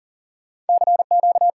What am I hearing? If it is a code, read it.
Morse code, heard as a keyed tone, about 56 wpm, spaced as for 48 wpm, 700 Hz. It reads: CQ
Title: Speed-Racing Common QSO Phrases Encore - 40wpm